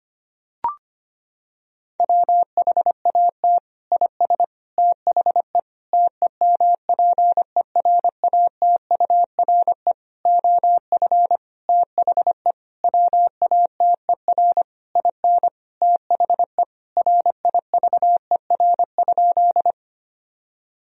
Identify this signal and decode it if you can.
Morse code, heard as a keyed tone, about 25 wpm, 700 Hz. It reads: WHAT IS THE TEMPERATURE OF THE WATER IN THE RIVER?
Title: Sentences from Top 1000 Words - Review Part 4 - 25wpm